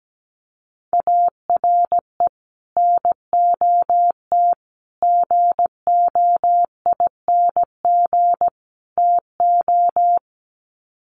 Morse code, heard as a keyed tone, about 17 wpm, 700 Hz. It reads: ARE NOT GOING TO
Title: Sets of 4 Words 17wpm